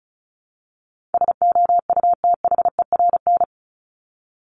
Morse code, heard as a keyed tone, about 34 wpm, 700 Hz. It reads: SOUTHERN
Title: Top 1751-2000 Words 35wpm